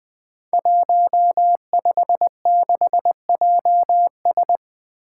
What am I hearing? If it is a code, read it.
Morse code, heard as a keyed tone, about 20 wpm, 700 Hz. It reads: 156JS
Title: Five Challenging Letter-Number Combinations 20wpm